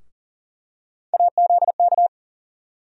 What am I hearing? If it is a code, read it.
Morse code, heard as a keyed tone, about 40 wpm, 700 Hz. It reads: AZK
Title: USA Call Sign Postfixes - 3 Letters 40wpm